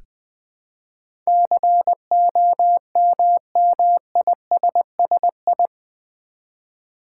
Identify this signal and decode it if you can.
Morse code, heard as a keyed tone, about 20 wpm, 700 Hz. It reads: COMMISSI
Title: Build A Word - 4 Letters At A Time - 12-Letter Words - 20wpm